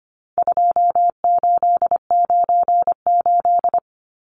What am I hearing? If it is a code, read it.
Morse code, heard as a keyed tone, about 25 wpm, 700 Hz. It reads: 2898
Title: Numbers - 4-Digit 25wpm